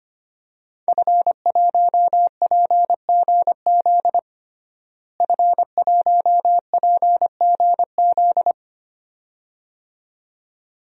Morse code, heard as a keyed tone, about 25 wpm, 700 Hz. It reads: F1PGZ F1PGZ
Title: DXCC-100 Call Signs - 5 Characters - Repeated 2x - 25wpm